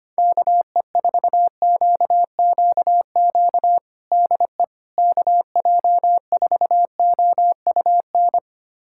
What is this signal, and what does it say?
Morse code, heard as a keyed tone, about 25 wpm, 700 Hz. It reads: KE4QQQ DE KJ4OUN
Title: USA Callsign DE Callsign - 6 Characters - 25wpm